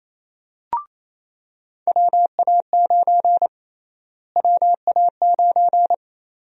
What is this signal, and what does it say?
Morse code, heard as a keyed tone, about 27 wpm, 700 Hz. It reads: WA9 WA9
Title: USA Call Sign Prefixes - Repeated 2x - 28wpm